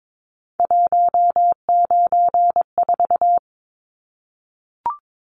Morse code, heard as a keyed tone, about 22 wpm, 700 Hz. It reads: 194
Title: Numbers - 3-Digit 22wpm